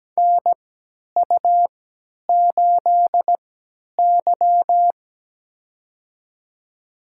Morse code, heard as a keyed tone, about 17 wpm, spaced as for 10 wpm, 700 Hz. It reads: NU8Y
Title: USA Call Signs - 4 Characters - 3x Character Spacing - 17wpm